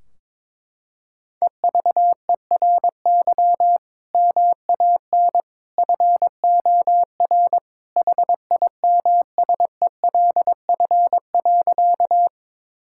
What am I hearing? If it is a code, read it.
Morse code, heard as a keyed tone, about 22 wpm, 700 Hz. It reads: EVERY MAN FOR HIMSELF.
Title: Sentences from Top 1600 Words 22wpm